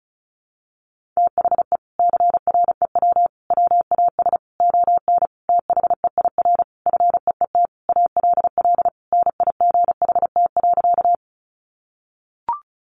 Morse code, heard as a keyed tone, about 34 wpm, 700 Hz. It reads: THE CREW WAS ON THEIR FEET ALL NIGHT.
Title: Sentences from Top 1500 Words 35wpm